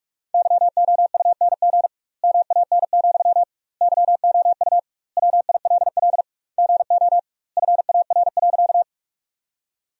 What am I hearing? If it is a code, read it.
Morse code, heard as a keyed tone, about 42 wpm, 700 Hz. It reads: YOUNG MAN, YOU WILL GO FAR.
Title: 2x Word-Spacing - Sentences from Top 400 Words - 45wpm